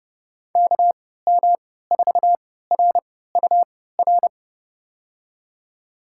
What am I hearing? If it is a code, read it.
Morse code, heard as a keyed tone, about 30 wpm, spaced as for 17 wpm, 700 Hz. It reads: KM4RUR
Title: USA Call Signs - 6 Characters - 3x Character Spacing - 30wpm